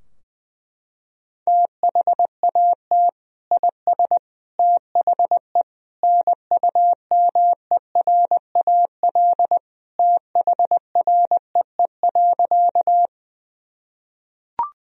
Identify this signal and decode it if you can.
Morse code, heard as a keyed tone, about 20 wpm, 700 Hz. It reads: THAT IS THE NUMERAL THREE.
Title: Sentences from Top 1000 Words - Review Part 2 - 20wpm